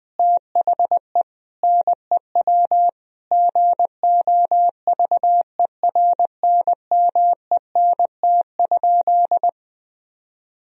Morse code, heard as a keyed tone, about 20 wpm, 700 Hz. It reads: THE NEW GOVERNMENT?
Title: Sentences from Top 1000 Words - Review Part 3 - 20wpm